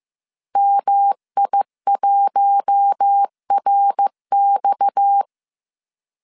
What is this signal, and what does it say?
Morse code, heard as a keyed tone, about 15 wpm, 800 Hz. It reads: MI1RX